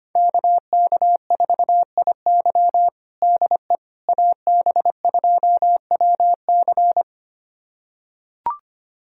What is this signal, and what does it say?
Morse code, heard as a keyed tone, about 25 wpm, 700 Hz. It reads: KK4IY DE AB2WC
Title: USA Callsign DE Callsign - 5 Characters - 25wpm